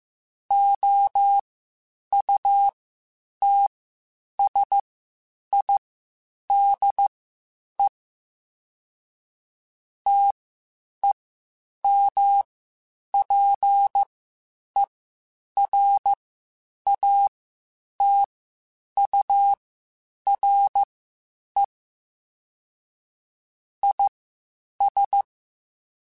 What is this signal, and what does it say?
Morse code, heard as a keyed tone, about 15 wpm, spaced as for 8 wpm, 800 Hz. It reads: OUTSIDE TEMPERATURE IS
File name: Spaced Characters / Spaced Characters-mp3s / Weather 3 17 sp @13